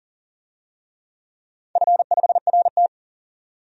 Morse code, heard as a keyed tone, about 40 wpm, 700 Hz. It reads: F5RT
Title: DXCC-100 Call Signs - 4 Characters - 40wpm